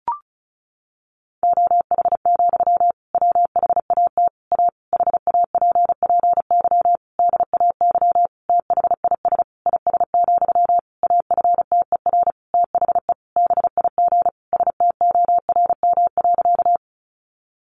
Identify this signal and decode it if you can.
Morse code, heard as a keyed tone, about 34 wpm, 700 Hz. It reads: OH, WHAT A HAPPY DAY THIS IS, AFTER THE BIG STORM.
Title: Sentences from Top 1500 Words 35wpm